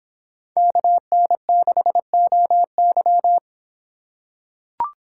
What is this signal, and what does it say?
Morse code, heard as a keyed tone, about 25 wpm, 700 Hz. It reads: KN6OY